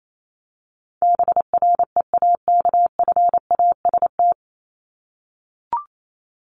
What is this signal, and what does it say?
Morse code, heard as a keyed tone, about 28 wpm, 700 Hz. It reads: BREAKFAST
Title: Top 1251-1500 Words 28wpm